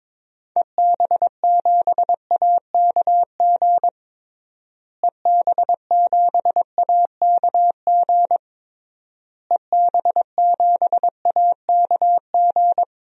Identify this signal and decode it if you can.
Morse code, heard as a keyed tone, about 21 wpm, 700 Hz. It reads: EB7AKG EB7AKG EB7AKG
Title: DXCC-100 Call Signs - 6 Characters - Repeated 3x - 22wpm